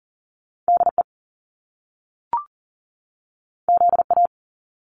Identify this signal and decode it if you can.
Morse code, heard as a keyed tone, about 40 wpm, 700 Hz. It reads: DE ZA
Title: Top Two Letter Combinations in Words 40wpm